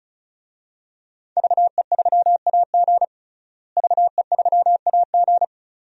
Morse code, heard as a keyed tone, about 36 wpm, 700 Hz. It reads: VE3AG VE3AG
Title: DXCC-100 Call Signs - 5 Characters - Repeated 2x - 35wpm